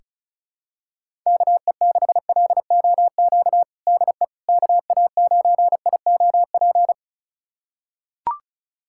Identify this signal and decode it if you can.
Morse code, heard as a keyed tone, about 36 wpm, 700 Hz. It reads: KE6LOQ DE KA9IOP
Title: USA Callsign DE Callsign 35wpm